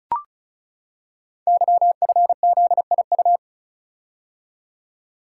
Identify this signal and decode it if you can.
Morse code, heard as a keyed tone, about 36 wpm, 700 Hz. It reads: YFZIU
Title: Code Groups - 5 Letters - 35wpm